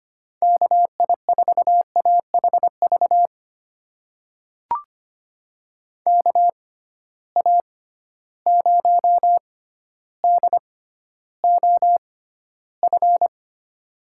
Morse code, heard as a keyed tone, about 25 wpm, spaced as for 13 wpm, 700 Hz. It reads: KI4AHV KA0DOF
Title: USA Call Signs - 6 Characters - 6x Character Spacing - 25wpm